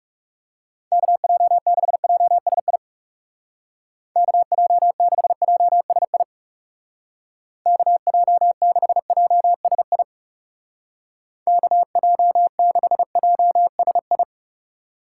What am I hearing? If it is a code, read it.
Morse code, heard as a keyed tone, about 36 wpm, 700 Hz. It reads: KJ6JSI KJ6JSI KJ6JSI KJ6JSI
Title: Speed Racing - USA 2x3 Call Signs 30wpm